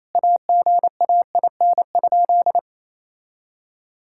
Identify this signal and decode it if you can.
Morse code, heard as a keyed tone, about 28 wpm, 700 Hz. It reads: AGAIN?
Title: Sentences from Top 1600 Words 28wpm